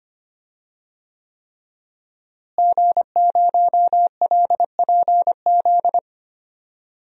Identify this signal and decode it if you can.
Morse code, heard as a keyed tone, about 25 wpm, 700 Hz. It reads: G0LPZ
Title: DXCC-100 Call Signs - 5 Characters - Repeated 3x - 25wpm